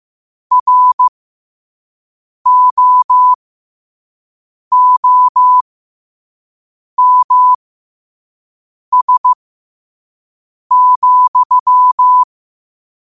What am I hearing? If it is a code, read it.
Morse code, heard as a keyed tone, about 15 wpm, spaced as for 5 wpm, 1000 Hz. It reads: ROOMS,
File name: qotdfiles / 5wpm